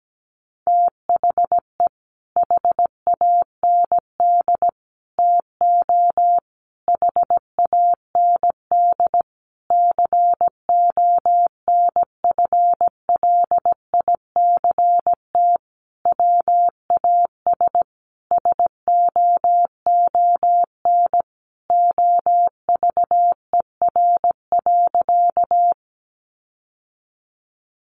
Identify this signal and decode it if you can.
Morse code, heard as a keyed tone, about 17 wpm, 700 Hz. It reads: THE HAND TO HAND CONFLICT WAS SOON OVER.
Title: Sentences from Top 1300 Words 17wpm